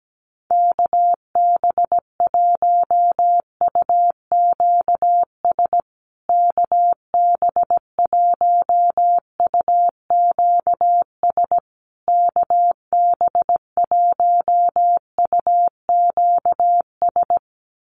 Morse code, heard as a keyed tone, about 17 wpm, 700 Hz. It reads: KB1UQS KB1UQS KB1UQS
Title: USA Call Signs Repeated 3x 17wpm